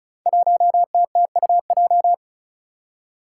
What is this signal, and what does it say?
Morse code, heard as a keyed tone, about 36 wpm, 700 Hz. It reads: 1TTUJ
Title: Five Challenging Letter-Number Combinations 35wpm